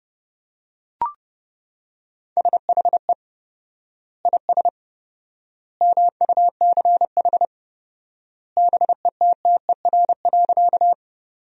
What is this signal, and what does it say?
Morse code, heard as a keyed tone, about 32 wpm, 700 Hz. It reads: SHE IS MUCH BETTER.
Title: 4x Word-Spacing - Sentences from Top 400 Words - 30wpm